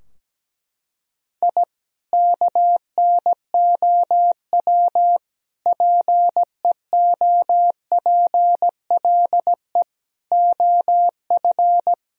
Morse code, heard as a keyed tone, about 17 wpm, 700 Hz. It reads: I KNOW PEOPLE OF
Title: Sentences from Top 1000 Words - Review Part 4 - 17wpm